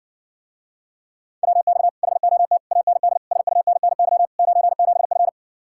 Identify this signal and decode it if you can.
Morse code, heard as a keyed tone, about 59 wpm, 700 Hz. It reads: WX HOT AND SUNNY 97F